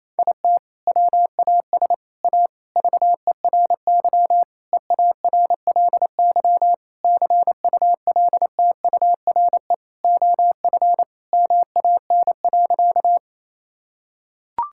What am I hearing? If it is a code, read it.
Morse code, heard as a keyed tone, about 28 wpm, 700 Hz. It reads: IT WAS A VERY EARLY CULTURE OF MAN.
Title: Sentences from Top 1000 Words - Review Part 6 - 28wpm